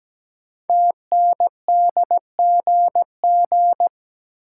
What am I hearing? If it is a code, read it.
Morse code, heard as a keyed tone, about 17 wpm, 700 Hz. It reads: TNDGG
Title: Code Groups - 5 Letters - 17wpm